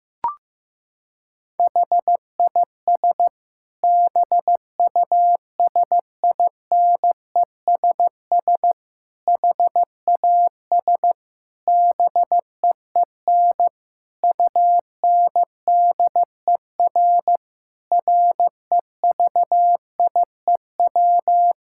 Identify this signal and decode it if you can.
Morse code, heard as a keyed tone, about 15 wpm, 700 Hz. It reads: HIS BUSINESS HAS BEEN UNDER REVIEW
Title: Sentences from Top 1000 Words - Review Part 6 - 15wpm